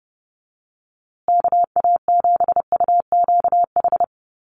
Morse code, heard as a keyed tone, about 30 wpm, 700 Hz. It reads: KA7UQH
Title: USA Call Signs Repeated 2x 30wpm